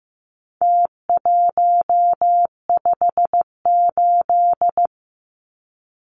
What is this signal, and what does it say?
Morse code, heard as a keyed tone, about 15 wpm, 700 Hz. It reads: T158
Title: Numbers - Contest Style - 4-Digit 15wpm